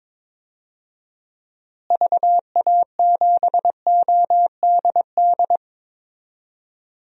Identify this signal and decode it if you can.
Morse code, heard as a keyed tone, about 22 wpm, 700 Hz. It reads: VA7ODD
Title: DXCC-100 Call Signs - 6 Characters - Repeated 3x - 22wpm